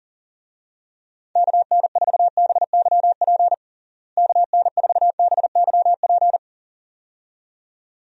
Morse code, heard as a keyed tone, about 40 wpm, 700 Hz. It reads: KN4BYP KN4BYP
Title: USA Call Signs - 6 Characters - Repeated 2x - 40wpm